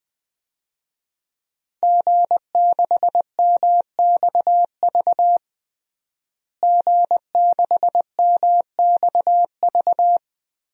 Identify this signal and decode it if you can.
Morse code, heard as a keyed tone, about 21 wpm, 700 Hz. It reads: G6MXV G6MXV
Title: DXCC-100 Call Signs - 5 Characters - Repeated 3x - 20wpm